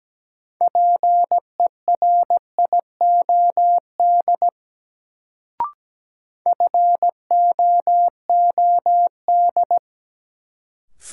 Morse code, heard as a keyed tone, about 17 wpm, 700 Hz. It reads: PERIOD FOOD
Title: Sets of 5 Words - New Word Drill - 17wpm